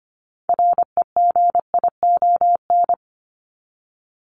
Morse code, heard as a keyed tone, about 25 wpm, 700 Hz. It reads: REGION
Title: Top 1251-1500 Words 25wpm